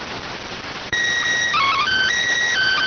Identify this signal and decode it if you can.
SSTV (slow-scan television), PD 290.